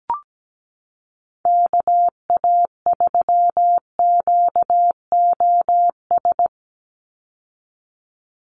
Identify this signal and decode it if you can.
Morse code, heard as a keyed tone, about 17 wpm, 700 Hz. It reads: KA3QOS
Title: USA Call Signs 17wpm